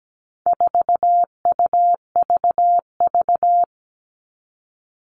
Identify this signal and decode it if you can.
Morse code, heard as a keyed tone, about 17 wpm, 700 Hz. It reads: 4UVV
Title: All Cut Numbers - 4-Digit 17wpm